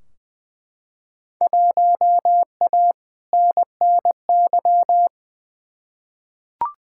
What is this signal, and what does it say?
Morse code, heard as a keyed tone, about 20 wpm, 700 Hz. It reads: 1A NNY
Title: Field Day Exchanges Encore - 20wpm